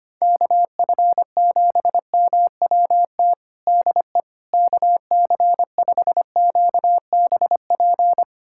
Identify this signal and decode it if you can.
Morse code, heard as a keyed tone, about 25 wpm, 700 Hz. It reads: KF7MWT DE KC5QBP
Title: USA Callsign DE Callsign - 6 Characters - 25wpm